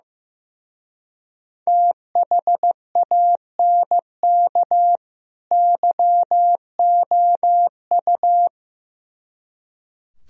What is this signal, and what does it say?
Morse code, heard as a keyed tone, about 16 wpm, 700 Hz. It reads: THANK YOU
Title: Speed-Racing - Sets of 2 Words 15wpm